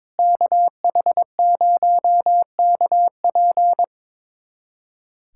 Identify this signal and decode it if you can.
Morse code, heard as a keyed tone, about 22 wpm, 700 Hz. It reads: KH0KP
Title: USA Callsign DE Callsign 22wpm